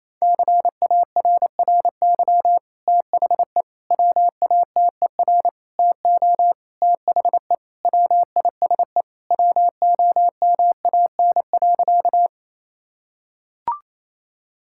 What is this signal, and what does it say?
Morse code, heard as a keyed tone, about 27 wpm, 700 Hz. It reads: CARRY THE WATER TO THE WISE WOMAN.
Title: Sentences from Top 1600 Words 28wpm